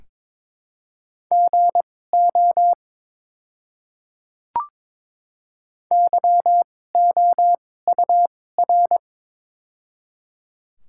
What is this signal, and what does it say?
Morse code, heard as a keyed tone, about 22 wpm, spaced as for 16 wpm, 700 Hz. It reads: GO YOUR
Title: 2x Character Spacing - Top 100 Words - 22wpm